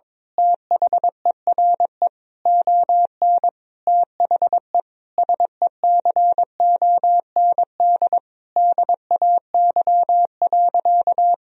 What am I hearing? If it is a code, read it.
Morse code, heard as a keyed tone, about 22 wpm, 700 Hz. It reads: THERE ON THE SECOND DAY.
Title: Sentences from Top 1000 Words - Review Part 4 - 22wpm